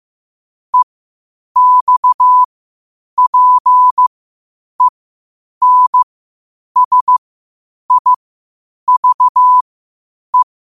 Morse code, heard as a keyed tone, about 15 wpm, spaced as for 8 wpm, 1000 Hz. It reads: EXPENSIVE
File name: qotdfiles / 8wpm